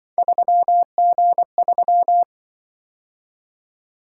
Morse code, heard as a keyed tone, about 24 wpm, 700 Hz. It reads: 3G3
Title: Speed-Racing - Three Letter-Number Combinations - 20wpm